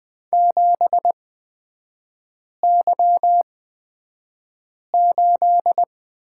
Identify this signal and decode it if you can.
Morse code, heard as a keyed tone, about 20 wpm, spaced as for 5 wpm, 700 Hz. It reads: 7Y8
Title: 236 Lesson 38e - Letter q - Three Characters 20wpm